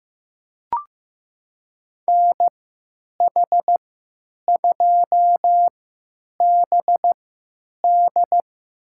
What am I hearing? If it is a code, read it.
Morse code, heard as a keyed tone, about 15 wpm, spaced as for 9 wpm, 700 Hz. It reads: NH2BD
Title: USA Call Signs - 5 Characters - 3x Character Spacing - 15wpm